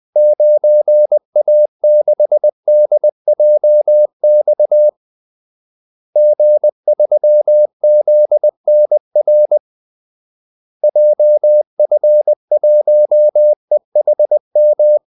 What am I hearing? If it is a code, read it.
Morse code, heard as a keyed tone, about 20 wpm, 600 Hz. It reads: 9A6DJX G3ZNR JF1EHM
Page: Kaldesignaler 20 wpm